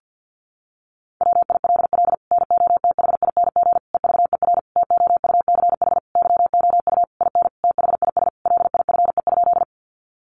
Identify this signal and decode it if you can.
Morse code, heard as a keyed tone, about 50 wpm, 700 Hz. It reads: WILL NOTHING EVER TOUCH YOU IN THIS LIFE?
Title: Sentences from Top 1200 Words 50wpm